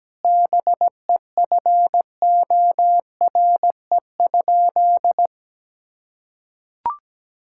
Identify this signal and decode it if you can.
Morse code, heard as a keyed tone, about 17 wpm, 700 Hz. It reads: BEFORE?
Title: Sentences from Top 1000 Words - Review Part 6 - 17wpm